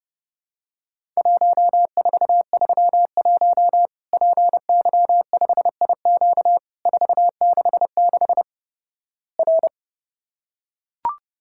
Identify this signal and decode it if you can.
Morse code, heard as a keyed tone, about 30 wpm, 700 Hz. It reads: 1431 PY5IQ 466 R
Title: Worked All Europe DX Contest - QTC Exchange - 30wpm